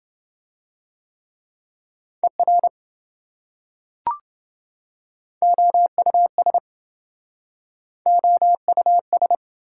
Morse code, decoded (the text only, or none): ER OUS OUS